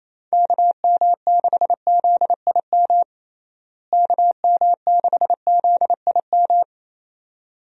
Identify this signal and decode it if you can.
Morse code, heard as a keyed tone, about 27 wpm, 700 Hz. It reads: KM6ZIM KM6ZIM
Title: USA Call Signs - 6 Characters - Repeated 2x - 28wpm